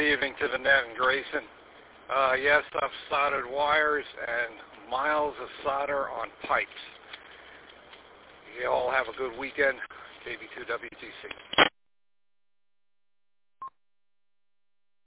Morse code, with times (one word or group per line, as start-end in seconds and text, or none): none